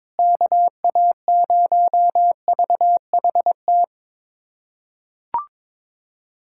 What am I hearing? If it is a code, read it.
Morse code, heard as a keyed tone, about 22 wpm, 700 Hz. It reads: KA0VHT
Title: USA Callsign DE Callsign - 6 Characters - 22wpm